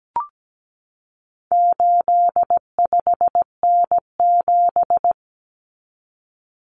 Morse code, heard as a keyed tone, about 17 wpm, 700 Hz. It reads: 85N7
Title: Common Cut Numbers - 4-Digit 17wpm